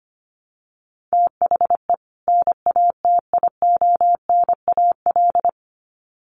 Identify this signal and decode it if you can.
Morse code, heard as a keyed tone, about 25 wpm, 700 Hz. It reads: THE NATIONAL
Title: Sets of 2 Words 25wpm